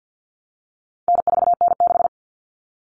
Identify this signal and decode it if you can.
Morse code, heard as a keyed tone, about 50 wpm, 700 Hz. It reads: N4N6
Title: Common Cut Numbers - 4-Digit 50wpm